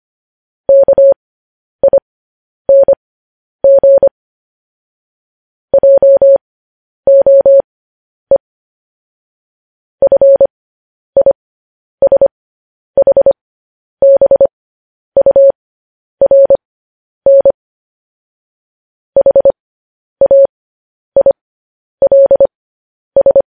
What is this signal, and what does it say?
Morse code, heard as a keyed tone, about 25 wpm, spaced as for 10 wpm, 550 Hz. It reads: KING JOE FISHBURN HAILS
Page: News Headlines in Morse Code at 10 WPM